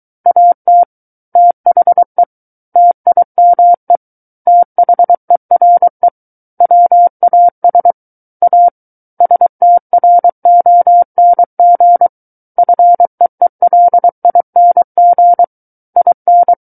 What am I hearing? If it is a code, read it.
Morse code, heard as a keyed tone, about 23 wpm, 700 Hz. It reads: AT THE TIME THERE WAS A STRONG FEELING IN
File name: Book One / 14-Chapter 14 - 23 WPM